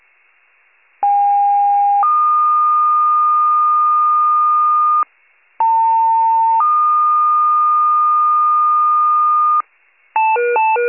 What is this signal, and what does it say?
Two-tone page, heard as a one-second tone, then a three-second tone, about 810 then 1220 Hz, about 900 then 1220 Hz. It is heard from 1.0 to 9.6 s.